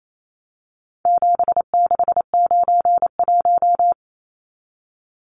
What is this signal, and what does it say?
Morse code, heard as a keyed tone, about 28 wpm, 700 Hz. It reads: 7691
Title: Numbers - 4-Digit 28wpm